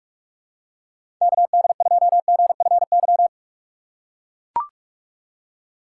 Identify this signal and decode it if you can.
Morse code, heard as a keyed tone, about 41 wpm, 700 Hz. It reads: KD2GFY
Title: USA Call Signs - 6 Characters - 3x Character Spacing - 45wpm